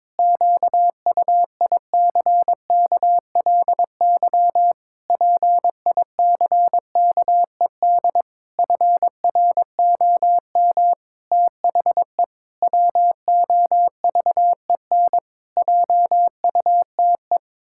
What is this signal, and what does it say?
Morse code, heard as a keyed tone, about 22 wpm, 700 Hz. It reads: QUICKLY PICKED FROM THE WOVEN JUTE
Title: Pangram Challenge v2 - 22wpm